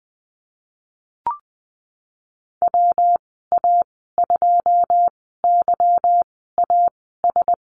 Morse code, heard as a keyed tone, about 20 wpm, spaced as for 15 wpm, 700 Hz. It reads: WA2YAS
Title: 147 Lesson 24g - Letter y - Call Signs 20wpm